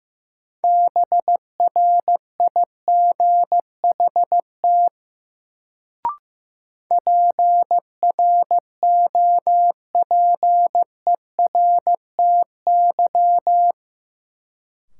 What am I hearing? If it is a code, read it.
Morse code, heard as a keyed tone, about 15 wpm, 700 Hz. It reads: BRIGHT PROPERTY
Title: Sentences from Top 1000 Words - New Word Drill - 15wpm